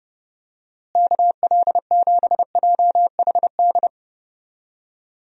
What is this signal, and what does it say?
Morse code, heard as a keyed tone, about 32 wpm, 700 Hz. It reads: KL7JHD
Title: USA Call Signs - 6 Characters - Repeated 3x - 30wpm